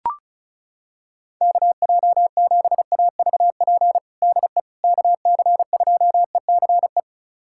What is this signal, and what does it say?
Morse code, heard as a keyed tone, about 36 wpm, 700 Hz. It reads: KJ7AVP DE KC2ECE
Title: USA Callsign DE Callsign 35wpm